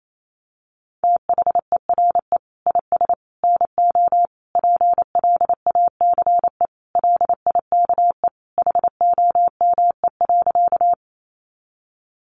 Morse code, heard as a keyed tone, about 28 wpm, 700 Hz. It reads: THERE IS NO PLACE LIKE HOME.
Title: Sentences from Top 200 Words 28wpm